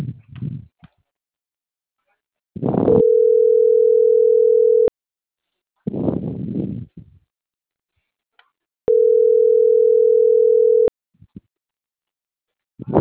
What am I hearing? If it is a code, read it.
Ringback tone (North America), 440 + 480 Hz.